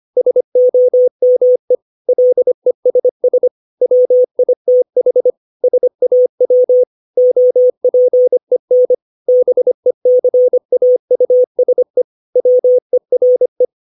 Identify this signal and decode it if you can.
Morse code, heard as a keyed tone, about 25 wpm, 500 Hz. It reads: SOME LESS WITH SAW OPEN BECAUSE WERE
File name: 632 words / 632 Words 25WPM